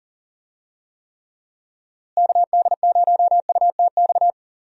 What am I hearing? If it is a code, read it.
Morse code, heard as a keyed tone, about 40 wpm, 700 Hz. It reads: KD0UTX
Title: USA Call Signs - 6 Characters - Repeated 2x - 40wpm